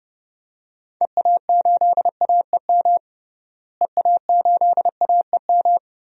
Morse code, heard as a keyed tone, about 30 wpm, 700 Hz. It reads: EA8AEM EA8AEM
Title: DXCC-100 Call Signs - 6 Characters - Repeated 3x - 30wpm